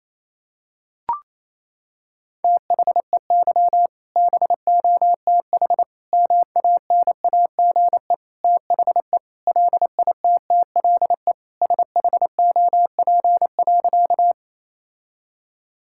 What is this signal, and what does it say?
Morse code, heard as a keyed tone, about 27 wpm, 700 Hz. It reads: THEY BOTH MANAGE THE LITTLE SHOP.
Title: Sentences from Top 1600 Words 28wpm